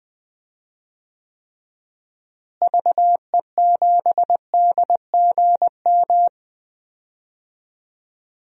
Morse code, heard as a keyed tone, about 20 wpm, 700 Hz. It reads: VE7DGM
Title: DXCC-100 Call Signs - 6 Characters - Repeated 2x - 20wpm